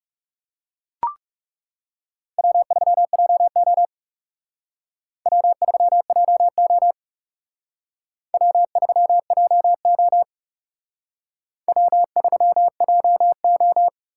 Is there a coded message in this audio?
Yes: Morse code from 2.4 to 13.9 s, heard as a keyed tone, about 38 wpm, 700 Hz. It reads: W3JO W3JO W3JO W3JO